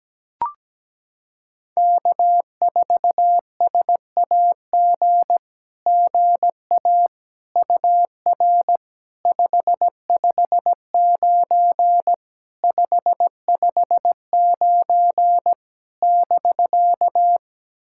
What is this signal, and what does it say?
Morse code, heard as a keyed tone, about 17 wpm, 700 Hz. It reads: K4SAG GA UR 559 559 <BK>
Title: SOTA QSO Practice 17wpm